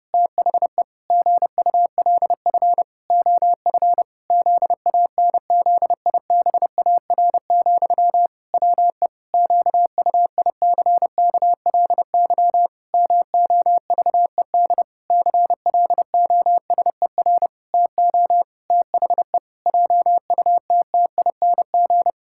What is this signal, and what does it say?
Morse code, heard as a keyed tone, about 32 wpm, 700 Hz. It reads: THE GULF OF ZANZIBAR, WE QUICKLY MOVED CLOSER TO THE JUTTING
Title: Pangram Challenge v2 - 30wpm